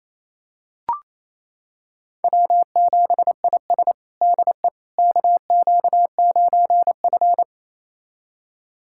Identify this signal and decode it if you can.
Morse code, heard as a keyed tone, about 28 wpm, 700 Hz. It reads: W7IS DE KQ9F
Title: USA Callsign DE Callsign - 4 Characters - 28wpm